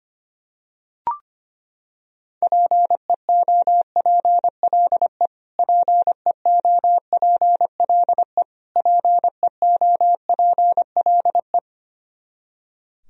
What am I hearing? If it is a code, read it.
Morse code, heard as a keyed tone, about 25 wpm, 700 Hz. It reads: PEOPLE PEOPLE PEOPLE
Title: Top 100 Words Repeated 25wpm